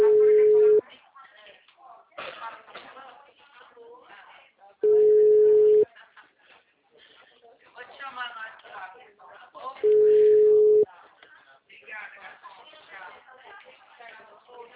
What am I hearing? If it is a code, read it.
Ringback tone (Europe, 425 Hz), with speech.
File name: out- -2019- - - .50910